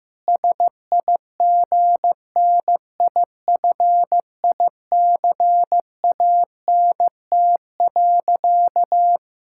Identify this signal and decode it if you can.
Morse code, heard as a keyed tone, about 15 wpm, 700 Hz. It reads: SIGNIFICANT.
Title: Sentences from Top 1000 Words - Review Part 6 - 15wpm